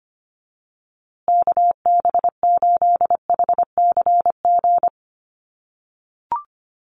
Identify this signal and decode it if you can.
Morse code, heard as a keyed tone, about 25 wpm, 700 Hz. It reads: KB8HCG
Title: USA Call Signs Repeated 2x 25wpm